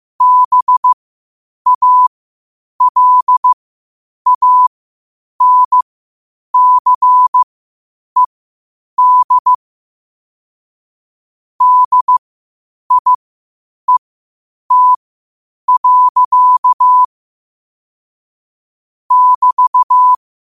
Morse code, heard as a keyed tone, about 15 wpm, spaced as for 8 wpm, 1000 Hz. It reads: BALANCED DIET. <BT>
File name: qotdfiles / 8wpm